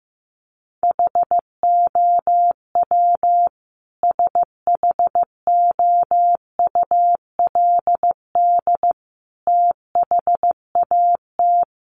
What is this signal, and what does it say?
Morse code, heard as a keyed tone, about 14 wpm, 700 Hz. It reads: HOW SHOULD THAT
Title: Sentences from Top 1300 Words 15wpm